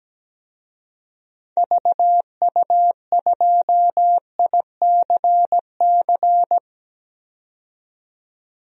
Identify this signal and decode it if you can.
Morse code, heard as a keyed tone, about 17 wpm, 700 Hz. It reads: VU2ICC
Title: DXCC-100 Call Signs - 6 Characters - 17wpm